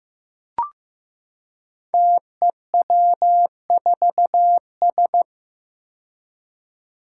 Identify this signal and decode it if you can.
Morse code, heard as a keyed tone, about 15 wpm, 700 Hz. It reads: TEW4S
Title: Five Challenging Letter-Number Combinations 15wpm